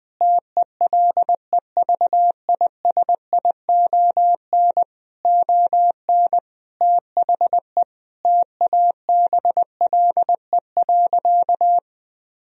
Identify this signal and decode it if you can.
Morse code, heard as a keyed tone, about 20 wpm, 700 Hz. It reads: TELEVISION ON THE TABLE.
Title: Sentences from Top 1000 Words - Review Part 4 - 20wpm